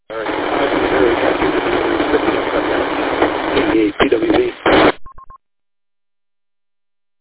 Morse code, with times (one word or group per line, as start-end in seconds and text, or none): none